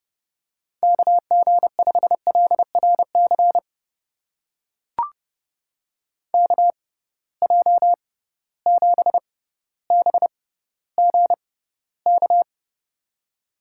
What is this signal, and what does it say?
Morse code, heard as a keyed tone, about 30 wpm, spaced as for 15 wpm, 700 Hz. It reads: KG5LRC KJ7BGK
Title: USA Call Signs - 6 Characters - 6x Character Spacing - 30wpm